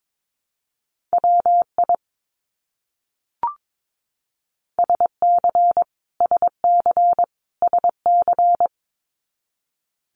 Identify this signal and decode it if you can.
Morse code, heard as a keyed tone, about 22 wpm, 700 Hz. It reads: WI SC SC SC
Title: US States Repeated 3x 22wpm